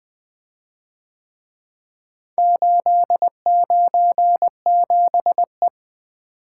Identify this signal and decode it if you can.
Morse code, heard as a keyed tone, about 24 wpm, 700 Hz. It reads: 897E
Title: Speed-Racing - Four Letter-Number Combinations - 20wpm